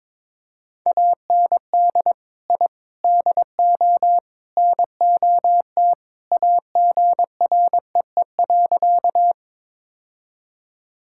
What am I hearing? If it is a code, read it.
Morse code, heard as a keyed tone, about 21 wpm, 700 Hz. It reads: AND I DO NOT AGREE.
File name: practice / 22wpm / Sentences with 4-Word Phrases - 22wpm